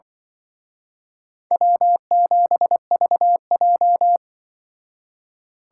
Morse code, heard as a keyed tone, about 24 wpm, 700 Hz. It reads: W7VJ
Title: Speed Racing - USA Call Signs 20wpm